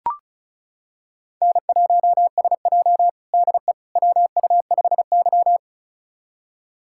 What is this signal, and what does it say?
Morse code, heard as a keyed tone, about 36 wpm, 690 Hz. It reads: N1SJ DE WU5Y
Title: USA Callsign DE Callsign - 4 Characters - 35wpm